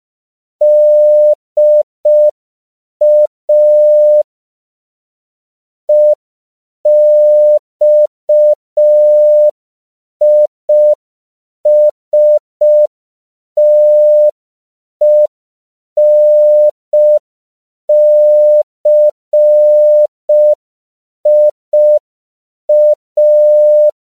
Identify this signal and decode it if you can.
Morse code, heard as a keyed tone, about 5 wpm, 600 Hz. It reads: DA EXISTENCIA
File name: CW / 5PPM